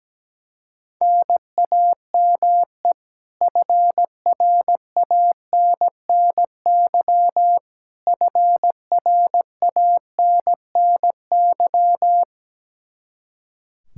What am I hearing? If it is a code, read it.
Morse code, heard as a keyed tone, about 17 wpm, 700 Hz. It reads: NAME FRANNY FRANNY
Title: Practice QSO - Exchange 1 - Beginning - 17wpm